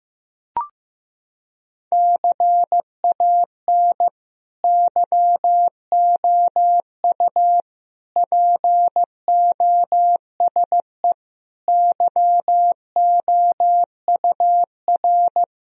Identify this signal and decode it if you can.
Morse code, heard as a keyed tone, about 16 wpm, 700 Hz. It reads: CAN YOU POSE YOUR
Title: Sentences from Top 1000 Words - Review Part 2 - 15wpm